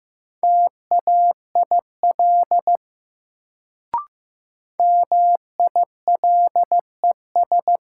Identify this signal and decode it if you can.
Morse code, heard as a keyed tone, about 15 wpm, 700 Hz. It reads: TAIL MILES
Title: Sentences from Top 1000 Words - Review Part 3 - New Word Drill - 15wpm